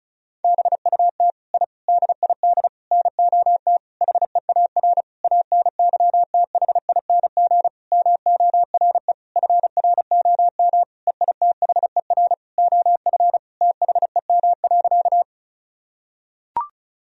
Morse code, heard as a keyed tone, about 36 wpm, 700 Hz. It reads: BUT I DID NOT HEAR ANYTHING MORE FROM EITHER OF THEM.
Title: Sentences from Top 1000 Words - Review Part 4 - 35wpm